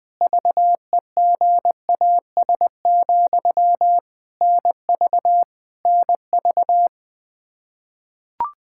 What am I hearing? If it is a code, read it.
Morse code, heard as a keyed tone, about 20 wpm, 700 Hz. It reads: VEGAS, NV NV
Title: K1USN SST QSO Practice 20wpm